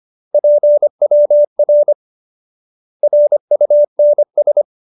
Morse code, heard as a keyed tone, about 25 wpm, 590 Hz. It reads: PWR RUNS